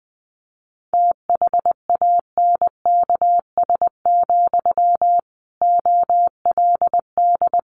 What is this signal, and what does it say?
Morse code, heard as a keyed tone, about 20 wpm, 700 Hz. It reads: THANKS, OLD
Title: Sentences from Top 600 Words 20wpm